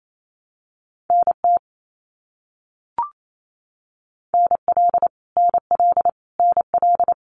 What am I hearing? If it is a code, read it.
Morse code, heard as a keyed tone, about 28 wpm, 700 Hz. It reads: NT NL NL NL
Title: Canadian Provinces and Territories - Repeated 3x - 28wpm